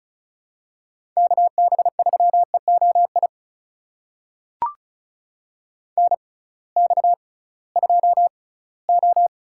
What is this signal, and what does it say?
Morse code, heard as a keyed tone, about 36 wpm, spaced as for 20 wpm, 700 Hz. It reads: KB3EOI NX2O
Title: USA Call Signs - 6x Character Spacing - 35wpm